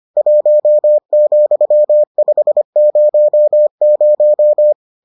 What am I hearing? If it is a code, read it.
Morse code, heard as a keyed tone, about 25 wpm, 600 Hz. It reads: 1,500